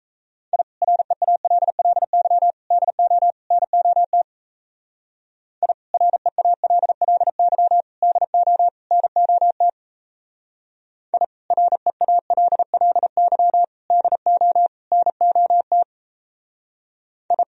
Morse code, heard as a keyed tone, about 32 wpm, 700 Hz. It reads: I REALLY DO NOT I REALLY DO NOT I REALLY DO NOT I
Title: Speed-Racing - Sets of 4 Words 28wpm